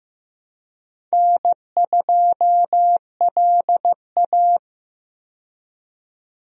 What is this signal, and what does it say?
Morse code, heard as a keyed tone, about 15 wpm, 700 Hz. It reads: N2LA
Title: USA Call Signs - 4 Characters - 15wpm